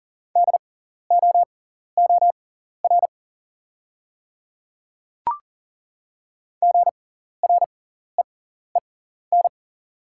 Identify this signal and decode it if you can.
Morse code, heard as a keyed tone, about 41 wpm, spaced as for 14 wpm, 700 Hz. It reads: DOOR GREEN
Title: 6x Character Spacing - Top 500 Words - 40wpm